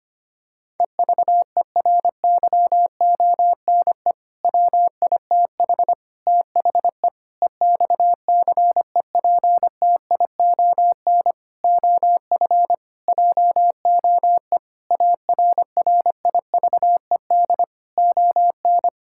Morse code, heard as a keyed tone, about 25 wpm, 700 Hz. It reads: EVERYONE WITH THE EXCEPTION OF JOE ARRIVED ON
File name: practice / 25wpm / Sentences with 4-Word Phrases - 25wpm